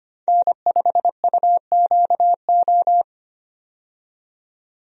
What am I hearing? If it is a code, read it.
Morse code, heard as a keyed tone, about 25 wpm, 700 Hz. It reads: N5UQO
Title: USA Call Signs - 5 Characters - 25wpm